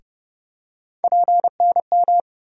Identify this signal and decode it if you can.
Morse code, heard as a keyed tone, about 30 wpm, 700 Hz. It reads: PNM
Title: USA Call Sign Postfixes - 3 Letters 30wpm